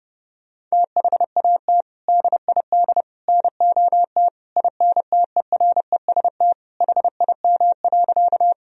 Morse code, heard as a keyed tone, about 29 wpm, 700 Hz. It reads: THAT DID NOT INTEREST HIM.
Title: Sentences from Top 1000 Words - Review Part 2 - 30wpm